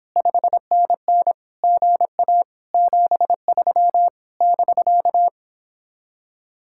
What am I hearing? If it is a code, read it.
Morse code, heard as a keyed tone, about 29 wpm, 700 Hz. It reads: 5NN GA 73 <BK>